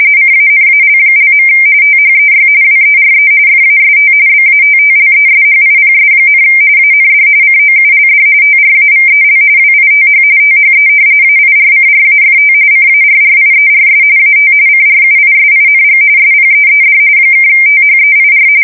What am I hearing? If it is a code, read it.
RTTY, about 45.45 baud. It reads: CANAL). THEY WERE FAST BATTLESHIPS, AND COULD TRAVEL WITH THE AIRCRAFT CARRIERS AT CRUISING SPEED (THEIR
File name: fsk170 / battleships